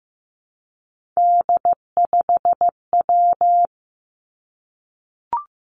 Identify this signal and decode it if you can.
Morse code, heard as a keyed tone, about 15 wpm, 700 Hz. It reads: D5W
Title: Three Challenging Letter-Number Combinations 15wpm